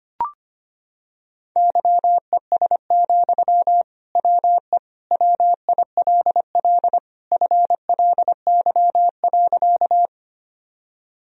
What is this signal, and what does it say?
Morse code, heard as a keyed tone, about 25 wpm, 700 Hz. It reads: YES, WE WILL FLY.
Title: Sentences from Top 1000 Words - Review Part 2 - 25wpm